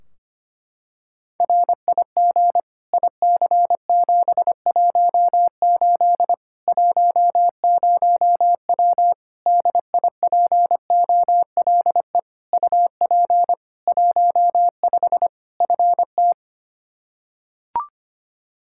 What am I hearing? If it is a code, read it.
Morse code, heard as a keyed tone, about 25 wpm, 700 Hz. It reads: RIG IC718 10W DIPOLE UP 15 FT
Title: Practice QSO - Exchange 2 - Info - 25wpm